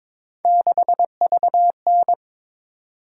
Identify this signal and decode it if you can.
Morse code, heard as a keyed tone, about 27 wpm, 700 Hz. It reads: 6VN
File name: Speed-Racing - Three Letter-Number Combinations - 22wpm